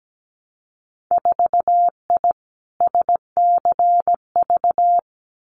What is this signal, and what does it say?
Morse code, heard as a keyed tone, about 17 wpm, 700 Hz. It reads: 4I SCV
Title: Winter Field Day Exchanges Encore 17wpm